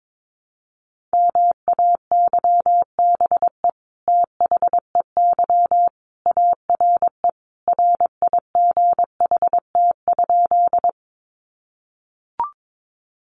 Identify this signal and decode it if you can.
Morse code, heard as a keyed tone, about 22 wpm, 700 Hz. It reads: MAYBE THEY ARE RIGHT?
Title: Sentences from Top 1000 Words 22wpm